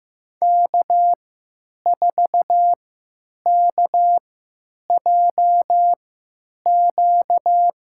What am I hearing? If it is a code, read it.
Morse code, heard as a keyed tone, about 15 wpm, spaced as for 9 wpm, 700 Hz. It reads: K4KJQ
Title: USA Call Signs - 3x Character Spacing - 15wpm